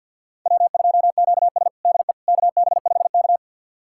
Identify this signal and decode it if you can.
Morse code, heard as a keyed tone, about 48 wpm, 700 Hz. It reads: W2QS DE KB5K
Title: USA Callsign DE Callsign - 4 Characters - 50wpm